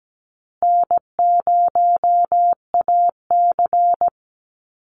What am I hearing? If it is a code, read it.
Morse code, heard as a keyed tone, about 17 wpm, 700 Hz. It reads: N0AC
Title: Easy Call Signs Repeated 3x - 17wpm